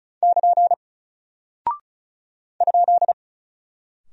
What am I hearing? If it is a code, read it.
Morse code, heard as a keyed tone, about 36 wpm, 700 Hz. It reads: <KN> ?
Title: Single Character (Letter:Number:Punctuation:Prosign) 35wpm